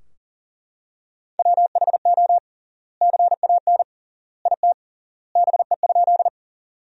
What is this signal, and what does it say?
Morse code, heard as a keyed tone, about 40 wpm, 700 Hz. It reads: WHO CAN IT BE?
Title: 3x Word-Spacing - Sentences from Top 100 Words - 40wpm